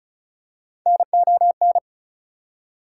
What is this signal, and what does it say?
Morse code, heard as a keyed tone, about 34 wpm, 700 Hz. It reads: NON
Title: Prefixes and Suffixes 35wpm